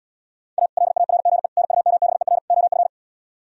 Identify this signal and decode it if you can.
Morse code, heard as a keyed tone, about 77 wpm, 700 Hz. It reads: A LARGE NUMBER OF